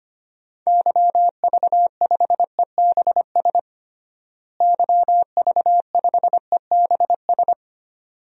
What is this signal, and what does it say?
Morse code, heard as a keyed tone, about 25 wpm, 700 Hz. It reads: YV5EBS YV5EBS
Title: DXCC-100 Call Signs - 6 Characters - Repeated 2x - 25wpm